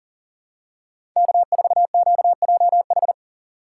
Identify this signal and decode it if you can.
Morse code, heard as a keyed tone, about 40 wpm, 700 Hz. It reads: K4QJH
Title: USA Call Signs - 5 Characters - 4x Character Spacing - 40wpm